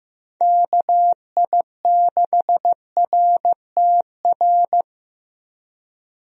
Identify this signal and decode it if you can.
Morse code, heard as a keyed tone, about 15 wpm, 700 Hz. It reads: KI6RTR
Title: USA Call Signs - 6 Characters - Repeated 2x - 15wpm